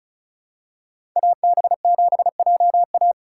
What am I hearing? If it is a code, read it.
Morse code, heard as a keyed tone, about 36 wpm, 700 Hz. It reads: AB7JA
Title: USA Call Signs - 5 Characters - 4x Character Spacing - 35wpm